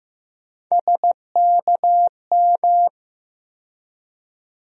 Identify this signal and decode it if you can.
Morse code, heard as a keyed tone, about 15 wpm, 700 Hz. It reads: SKM
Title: USA Call Sign Postfixes - 3 Letters 15wpm